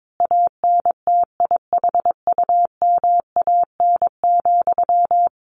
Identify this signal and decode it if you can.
Morse code, heard as a keyed tone, about 22 wpm, 700 Hz. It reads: ANTIHUMAN,
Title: A Sound Of Thunder 22wpm